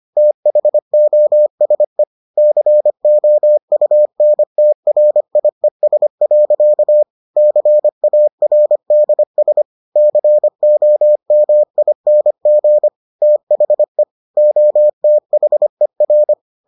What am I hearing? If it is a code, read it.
Morse code, heard as a keyed tone, about 25 wpm, 600 Hz. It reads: THOSE COUNTRIES. CARDS COMING THE OTHER